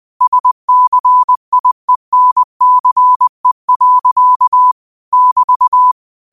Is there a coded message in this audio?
Yes: Morse code from 0.2 to 5.9 s, heard as a keyed tone, about 20 wpm, 1000 Hz. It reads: SCIENCE. <BT>